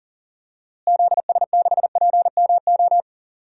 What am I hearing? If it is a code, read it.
Morse code, heard as a keyed tone, about 40 wpm, 700 Hz. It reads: ZS6PMO
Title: DXCC-100 Call Signs - 6 Characters - Repeated 2x - 40wpm